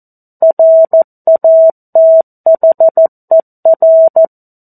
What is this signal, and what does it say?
Morse code, heard as a keyed tone, about 14 wpm, 650 Hz. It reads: RATHER
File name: 02-Chapter-02 14 WPM